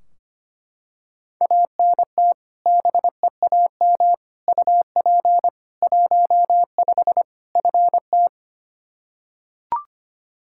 Morse code, heard as a keyed tone, about 25 wpm, 700 Hz. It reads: ANT BEAM UP 15 FT
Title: Practice QSO - Exchange 2 - Info - 25wpm